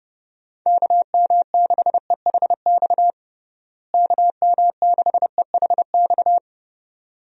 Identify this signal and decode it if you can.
Morse code, heard as a keyed tone, about 30 wpm, 700 Hz. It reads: KM6EHX KM6EHX
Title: USA Call Signs - 6 Characters - Repeated 2x - 30wpm